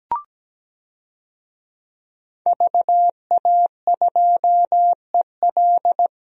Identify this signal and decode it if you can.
Morse code, heard as a keyed tone, about 17 wpm, 700 Hz. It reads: VA2EL
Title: DXCC-100 Call Signs - 5 Characters - Repeated 2x - 17wpm